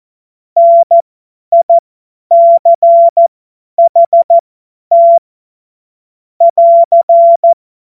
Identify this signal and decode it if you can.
Morse code, heard as a keyed tone, about 14 wpm, spaced as for 10 wpm, 680 Hz. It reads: NICHT <AR>
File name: Morsekurs / Witze / Witz 29 April 2025 50 F70